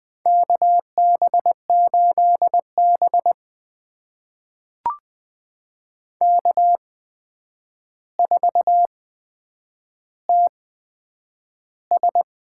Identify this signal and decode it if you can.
Morse code, heard as a keyed tone, about 20 wpm, spaced as for 9 wpm, 700 Hz. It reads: KB8B K4TS
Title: USA Call Signs - 8x Character Spacing - 20wpm